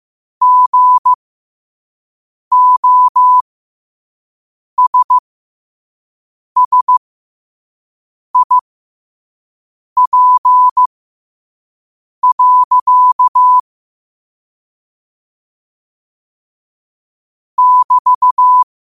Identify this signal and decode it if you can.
Morse code, heard as a keyed tone, about 15 wpm, spaced as for 5 wpm, 1000 Hz. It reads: GOSSIP. <BT>
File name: qotdfiles / 5wpm